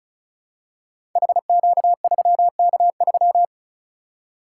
Morse code, heard as a keyed tone, about 36 wpm, 700 Hz. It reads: HQ3K3
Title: Code Groups - 5 Letter:Numbers - 35wpm